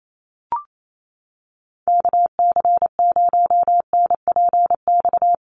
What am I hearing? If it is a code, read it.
Morse code, heard as a keyed tone, about 28 wpm, 700 Hz. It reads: KC0NPX
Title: USA Call Signs Repeated 2x 28wpm